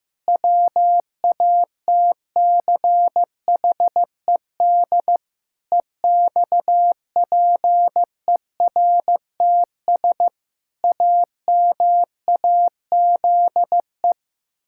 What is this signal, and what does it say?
Morse code, heard as a keyed tone, about 14 wpm, 700 Hz. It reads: WATCHED EXPERTS AMAZE
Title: Pangram Challenge v2 - 15wpm